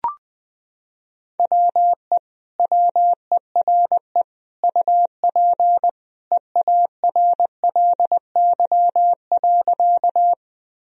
Morse code, heard as a keyed tone, about 21 wpm, 700 Hz. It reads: WE WERE UP EARLY.
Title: Sentences from Top 1000 Words - Review Part 2 - 20wpm